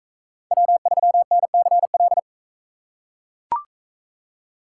Morse code, heard as a keyed tone, about 42 wpm, 700 Hz. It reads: W3NCL